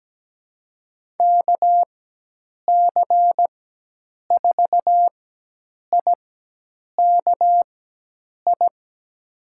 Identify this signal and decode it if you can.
Morse code, heard as a keyed tone, about 17 wpm, spaced as for 8 wpm, 700 Hz. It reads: KC4IKI
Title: USA Call Signs - 6 Characters - 4x Character Spacing - 17wpm